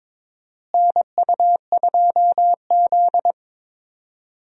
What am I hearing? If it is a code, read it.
Morse code, heard as a keyed tone, about 22 wpm, 700 Hz. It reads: NU2Z
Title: USA Call Signs - 4 Characters - Repeated 3x - 22wpm